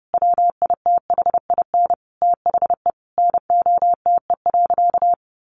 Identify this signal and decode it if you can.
Morse code, heard as a keyed tone, about 30 wpm, 700 Hz. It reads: WITHIN THE NOTE.
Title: Sentences from Top 1500 Words 30wpm